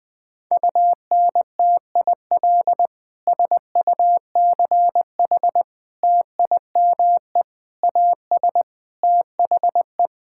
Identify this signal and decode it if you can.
Morse code, heard as a keyed tone, about 21 wpm, 700 Hz. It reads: UNTIL SUCH TIME AS THE
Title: Sentences with 9-Word Phrases - 20wpm